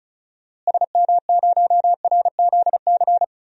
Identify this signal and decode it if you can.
Morse code, heard as a keyed tone, about 34 wpm, 700 Hz. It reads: SM0RZC
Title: DXCC-100 Call Signs - 6 Characters - Repeated 2x - 35wpm